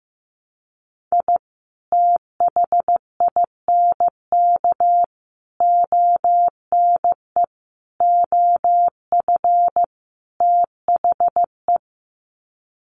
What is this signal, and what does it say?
Morse code, heard as a keyed tone, about 15 wpm, 700 Hz. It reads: I THINK ONE OF THE
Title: Sets of 5 Words 15wpm